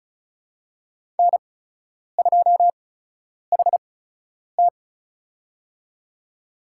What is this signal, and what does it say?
Morse code, heard as a keyed tone, about 34 wpm, spaced as for 10 wpm, 700 Hz. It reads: N2HT
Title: USA Call Signs - 8x Character Spacing - 35wpm